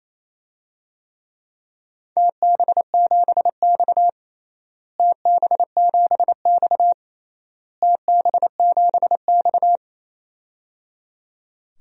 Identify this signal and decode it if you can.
Morse code, heard as a keyed tone, about 28 wpm, 700 Hz. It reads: TB7X TB7X TB7X
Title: DXCC-100 Call Signs - 4 Characters - Repeated 3x - 28wpm